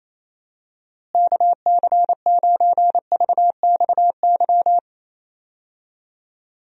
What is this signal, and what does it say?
Morse code, heard as a keyed tone, about 28 wpm, 700 Hz. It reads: KC9VXY
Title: USA Call Signs - 6 Characters - Repeated 2x - 28wpm